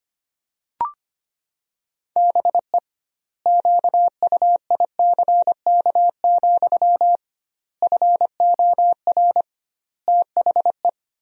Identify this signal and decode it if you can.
Morse code, heard as a keyed tone, about 25 wpm, 700 Hz. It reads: BE QUICK, FOR THE
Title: 2x Word-Spacing - Sentences from Top 500 Words - 25wpm